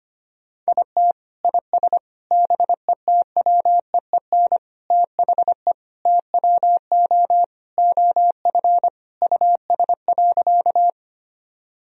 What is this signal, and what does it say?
Morse code, heard as a keyed tone, about 25 wpm, 700 Hz. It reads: IT IS BETWEEN THE TWO OF US.
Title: Sentences from Top 1000 Words - Review Part 1 - 25wpm